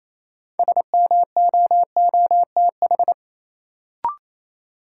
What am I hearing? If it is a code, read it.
Morse code, heard as a keyed tone, about 27 wpm, 700 Hz. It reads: SMOOTH
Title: Sentences from Top 1400 Words - New Word Drill Alt - 28wpm